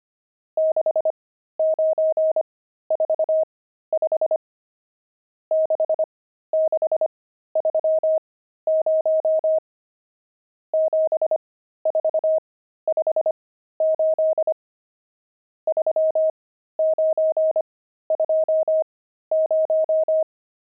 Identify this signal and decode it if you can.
Morse code, heard as a keyed tone, about 25 wpm, spaced as for 13 wpm, 630 Hz. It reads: 6945 6630 7458 3920
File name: pangram-fs-13-cs-25-nums13